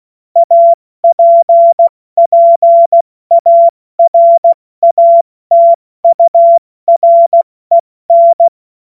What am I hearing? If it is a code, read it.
Morse code, heard as a keyed tone, about 16 wpm, 680 Hz. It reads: APPARATUREN
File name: Morsekurs / Witze / Witz 21. Dezember 2021, 70 F80